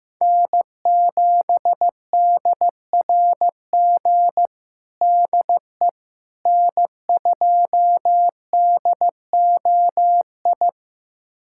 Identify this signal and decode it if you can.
Morse code, heard as a keyed tone, about 15 wpm, 700 Hz. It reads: N7DRG DE N2DOI
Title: USA Callsign DE Callsign - 5 Characters - 15wpm